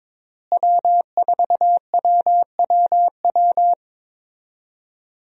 Morse code, heard as a keyed tone, about 22 wpm, 700 Hz. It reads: W4WWW
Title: USA Call Signs - 5 Characters - Repeated 2x - 22wpm